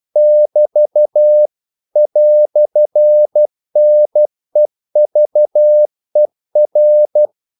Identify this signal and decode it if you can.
Morse code, heard as a keyed tone, about 12 wpm, 600 Hz. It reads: <BT> "NEVER
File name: CW / Practice / EnglishQuotations0112